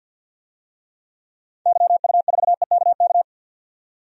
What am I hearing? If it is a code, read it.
Morse code, heard as a keyed tone, about 46 wpm, 700 Hz. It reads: YU4EKK
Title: DXCC-100 Call Signs - 6 Characters - 50wpm